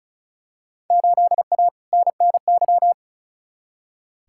Morse code, heard as a keyed tone, about 36 wpm, 700 Hz. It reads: 8A NNY
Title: Field Day Exchanges Encore - 35wpm